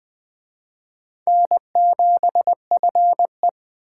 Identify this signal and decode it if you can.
Morse code, heard as a keyed tone, about 21 wpm, 700 Hz. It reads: N7FE